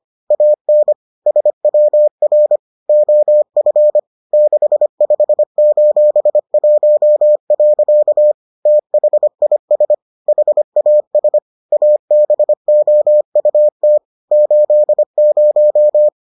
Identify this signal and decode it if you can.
Morse code, heard as a keyed tone, about 25 wpm, 600 Hz. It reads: AN SWR OF 65:1. THIS HAS ABOUT 80